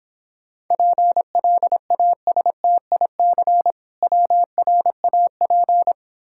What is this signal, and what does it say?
Morse code, heard as a keyed tone, about 26 wpm, 700 Hz. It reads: PLASTIC WRAP